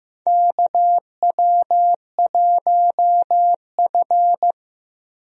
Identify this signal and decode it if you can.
Morse code, heard as a keyed tone, about 16 wpm, 700 Hz. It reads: KW1F
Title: USA Call Signs - 4 Characters - 15wpm